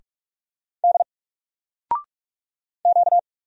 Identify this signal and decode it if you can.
Morse code, heard as a keyed tone, about 46 wpm, 700 Hz. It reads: D Q
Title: Single Letter-Number 45wpm